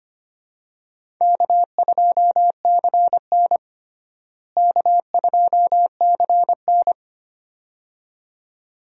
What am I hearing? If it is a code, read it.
Morse code, heard as a keyed tone, about 25 wpm, 700 Hz. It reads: K2CN K2CN
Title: USA Call Signs - 4 Characters - Repeated 2x - 25wpm